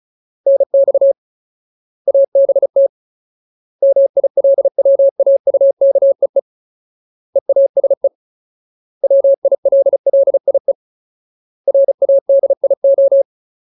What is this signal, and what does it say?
Morse code, heard as a keyed tone, about 34 wpm, 550 Hz. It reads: NX ABT MILWAUKEE EASE WILLIE RADIO
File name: Ultimate Word List / 35WPM Ultimate